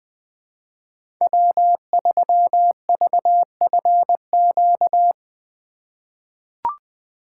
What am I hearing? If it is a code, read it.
Morse code, heard as a keyed tone, about 20 wpm, 700 Hz. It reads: W3VFQ
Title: USA Call Signs - 5 Characters - 3x Character Spacing - 20wpm